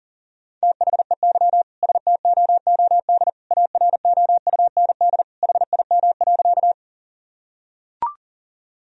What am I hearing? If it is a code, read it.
Morse code, heard as a keyed tone, about 40 wpm, 700 Hz. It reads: THEY STOOD AROUND HIM.
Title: Sentences from Top 1000 Words - Review Part 3 - 40wpm